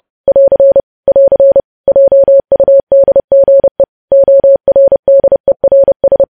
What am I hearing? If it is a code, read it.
Morse code, heard as a keyed tone, about 30 wpm, 550 Hz. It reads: <AR> <AR> JUDGE ORDERS
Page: news headlines in morse code at 30 wpm